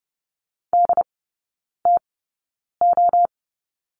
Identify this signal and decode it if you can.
Morse code, heard as a keyed tone, about 30 wpm, spaced as for 9 wpm, 700 Hz. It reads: DTO
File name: 081 Lesson 14e - Number 2 - Three Characters 30wpm